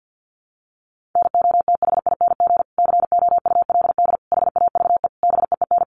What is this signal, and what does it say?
Morse code, heard as a keyed tone, about 50 wpm, 700 Hz. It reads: NOTHING COULD HAVE BEEN
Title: Sentences from Top 1300 Words 50wpm